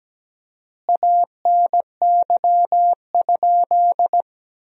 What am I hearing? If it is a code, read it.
Morse code, heard as a keyed tone, about 17 wpm, 700 Hz. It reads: ANY?
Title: 4x Word-Spacing - Sentences from Top 300 Words - 17wpm